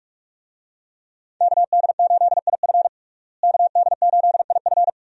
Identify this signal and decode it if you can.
Morse code, heard as a keyed tone, about 41 wpm, 700 Hz. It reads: KD8IF KD8IF
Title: USA Call Signs - 5 Characters - Repeated 2x - 45wpm